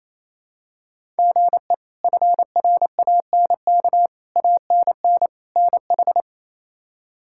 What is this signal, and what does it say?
Morse code, heard as a keyed tone, about 28 wpm, 700 Hz. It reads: GE FRANK ANN NH
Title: K1USN SST QSO Practice - 28wpm